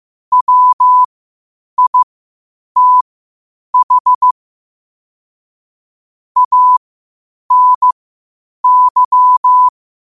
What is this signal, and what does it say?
Morse code, heard as a keyed tone, about 15 wpm, spaced as for 8 wpm, 1000 Hz. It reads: WITH ANY
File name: qotdfiles / 8wpm